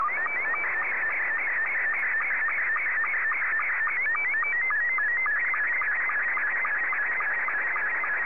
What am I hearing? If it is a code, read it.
SSTV (slow-scan television), Scottie 2.